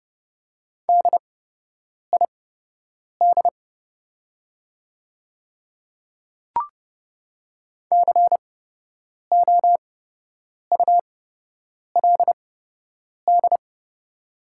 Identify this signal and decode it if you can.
Morse code, heard as a keyed tone, about 30 wpm, spaced as for 8 wpm, 690 Hz. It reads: DID COULD
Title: 8x Character Spacing - Top 200 Words - 30wpm